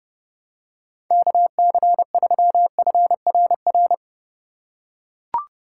Morse code, heard as a keyed tone, about 30 wpm, 700 Hz. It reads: KC3FRR
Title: USA Call Signs - 2x Character Spacing - 30wpm